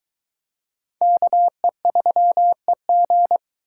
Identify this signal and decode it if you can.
Morse code, heard as a keyed tone, about 22 wpm, 700 Hz. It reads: KE3EG